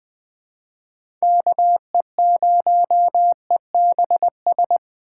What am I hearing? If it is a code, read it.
Morse code, heard as a keyed tone, about 24 wpm, 700 Hz. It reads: KE0EBS
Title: Speed Racing - USA 2x3 Call Signs 20wpm